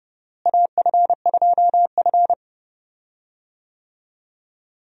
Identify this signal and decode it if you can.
Morse code, heard as a keyed tone, about 30 wpm, 700 Hz. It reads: AF2F
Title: USA Call Signs - 4 Characters - Repeated 3x - 30wpm